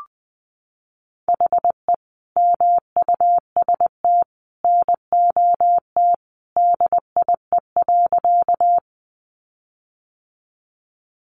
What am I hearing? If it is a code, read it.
Morse code, heard as a keyed tone, about 19 wpm, 700 Hz. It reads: HE MUST NOT DIE.
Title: Sentences from Top 1300 Words 20wpm